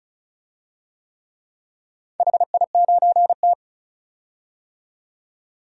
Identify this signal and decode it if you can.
Morse code, heard as a keyed tone, about 36 wpm, 700 Hz. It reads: HI9T
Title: DXCC-100 Call Signs - 4 Characters - 35wpm